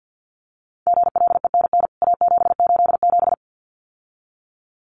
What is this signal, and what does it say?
Morse code, heard as a keyed tone, about 48 wpm, 700 Hz. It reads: GLENN A787
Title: CWT Exchange 50wpm